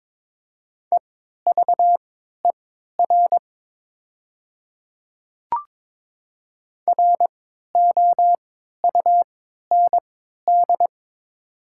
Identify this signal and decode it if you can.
Morse code, heard as a keyed tone, about 22 wpm, spaced as for 12 wpm, 700 Hz. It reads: EVER ROUND
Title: 3x Character Spacing - Top 500 Words - 22wpm